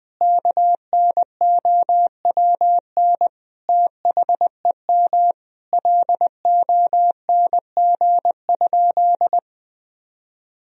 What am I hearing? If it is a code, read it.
Morse code, heard as a keyed tone, about 20 wpm, 700 Hz. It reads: KNOWN THEM LONG?
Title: Sentences from Top 1000 Words - Review Part 3 - 20wpm